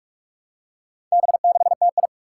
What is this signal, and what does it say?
Morse code, heard as a keyed tone, about 44 wpm, 700 Hz. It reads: B6TI
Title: Four Challenging Letter-Number Combinations 45wpm